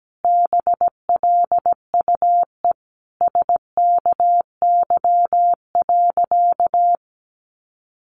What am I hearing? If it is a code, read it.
Morse code, heard as a keyed tone, about 17 wpm, 700 Hz. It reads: BLUE SKY.
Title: Sentences from Top 1300 Words 17wpm